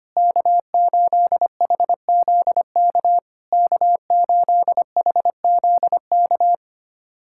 Morse code, heard as a keyed tone, about 25 wpm, 700 Hz. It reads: K8HZK K8HZK
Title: SOTA QSO Practice 25wpm